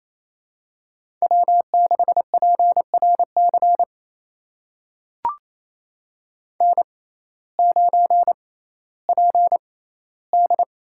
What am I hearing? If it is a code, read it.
Morse code, heard as a keyed tone, about 28 wpm, spaced as for 15 wpm, 700 Hz. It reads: W6PRC N9PD
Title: USA Call Signs - 5 Characters - 6x Character Spacing - 28wpm